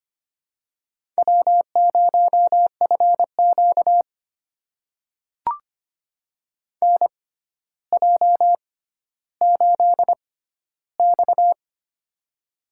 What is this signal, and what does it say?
Morse code, heard as a keyed tone, about 25 wpm, spaced as for 13 wpm, 700 Hz. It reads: W0FQ NJ8X
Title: USA Call Signs - 4 Characters - 6x Character Spacing - 25wpm